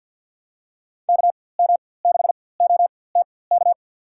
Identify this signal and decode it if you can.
Morse code, heard as a keyed tone, about 47 wpm, spaced as for 23 wpm, 690 Hz. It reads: KM6OTK
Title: USA Call Signs - 4x Character Spacing - 50wpm